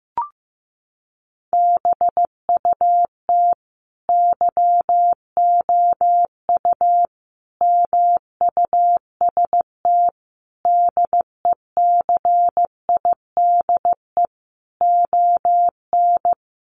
Morse code, heard as a keyed tone, about 15 wpm, 700 Hz. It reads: BUT YOU MUST DECIDE ON
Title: Sentences from Top 600 Words 15wpm